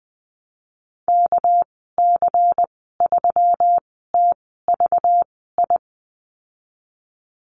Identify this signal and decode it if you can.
Morse code, heard as a keyed tone, about 19 wpm, spaced as for 14 wpm, 700 Hz. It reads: KC3TVI
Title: 219 Lesson 35g - Number 0 - Call Signs 20wpm